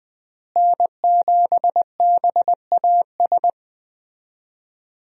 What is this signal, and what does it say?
Morse code, heard as a keyed tone, about 20 wpm, 700 Hz. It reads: N7BAS
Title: USA Call Signs - 5 Characters - Repeated 2x - 20wpm